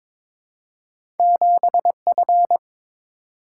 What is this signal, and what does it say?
Morse code, heard as a keyed tone, about 23 wpm, 700 Hz. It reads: 7F
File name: Two Letter-Number Combinations - 22wpm